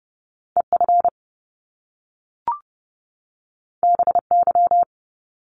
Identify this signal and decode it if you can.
Morse code, heard as a keyed tone, about 30 wpm, 700 Hz. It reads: EF BY
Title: Top Two Letter Combinations in Words 30wpm